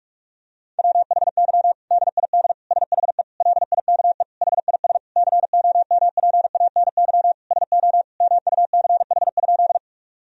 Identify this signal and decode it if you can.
Morse code, heard as a keyed tone, about 44 wpm, 700 Hz. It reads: WHY DID SHE LIKE HIS COMPANY SO MUCH?
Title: Sentences from Top 1000 Words - Review Part 3 - 45wpm